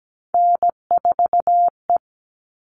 Morse code, heard as a keyed tone, about 17 wpm, 700 Hz. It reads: N4E
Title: All Cut Numbers - 3-Digit 17wpm